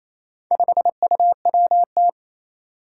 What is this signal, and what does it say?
Morse code, heard as a keyed tone, about 27 wpm, 700 Hz. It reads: 5UWT
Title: Four Challenging Letter-Number Combinations 28wpm